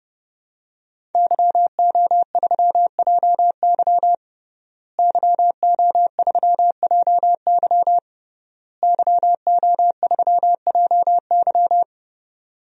Morse code, heard as a keyed tone, about 30 wpm, 700 Hz. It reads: YO3JY YO3JY YO3JY
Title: DXCC-100 Call Signs - 5 Characters - Repeated 3x - 30wpm